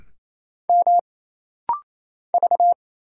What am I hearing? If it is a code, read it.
Morse code, heard as a keyed tone, about 27 wpm, 700 Hz. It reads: M V
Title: Single Letter-Number - Rapid-Fire 28wpm